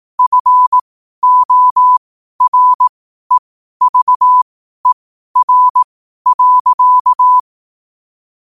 Morse code, heard as a keyed tone, about 18 wpm, spaced as for 12 wpm, 1000 Hz. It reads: FOREVER.
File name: qotdfiles / 12wpm